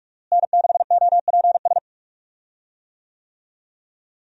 Morse code, heard as a keyed tone, about 45 wpm, 700 Hz. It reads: N6OPS
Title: USA Call Signs - 5 Characters - Repeated 2x - 45wpm